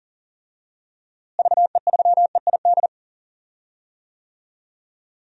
Morse code, heard as a keyed tone, about 40 wpm, 700 Hz. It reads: VE3EID
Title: DXCC-100 Call Signs - 6 Characters - Repeated 3x - 40wpm